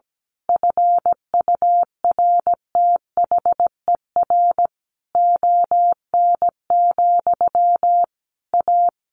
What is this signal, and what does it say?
Morse code, heard as a keyed tone, about 17 wpm, 700 Hz. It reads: FURTHER ON, A
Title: A Sound Of Thunder 17wpm